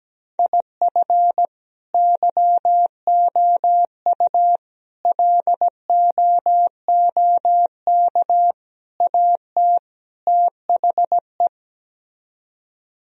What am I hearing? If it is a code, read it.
Morse code, heard as a keyed tone, about 17 wpm, 700 Hz. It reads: IF YOU LOOK AT THE
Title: Speed-Racing - Sets of 5 Words 17wpm